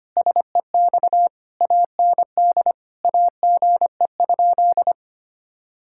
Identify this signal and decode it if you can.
Morse code, heard as a keyed tone, about 25 wpm, 700 Hz. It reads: SEX AND AGE?
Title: Sentences from Top 1600 Words 25wpm